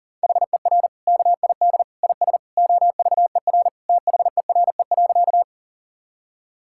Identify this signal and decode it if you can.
Morse code, heard as a keyed tone, about 37 wpm, 700 Hz. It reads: HER KID IS OVER THERE.
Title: Sentences from Top 1600 Words 40wpm